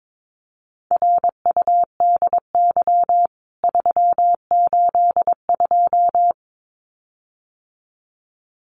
Morse code, heard as a keyed tone, about 22 wpm, 700 Hz. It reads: RUDY 382
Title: CWT Exchange 22wpm